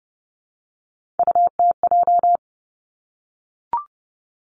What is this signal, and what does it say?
Morse code, heard as a keyed tone, about 30 wpm, 700 Hz. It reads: UTJ
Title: Three Challenging Letter-Number Combinations 30wpm